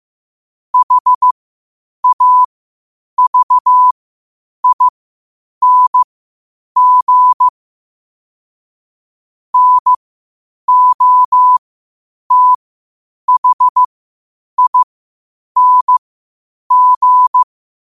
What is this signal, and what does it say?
Morse code, heard as a keyed tone, about 15 wpm, spaced as for 8 wpm, 1000 Hz. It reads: HAVING NOTHING
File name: qotdfiles / 8wpm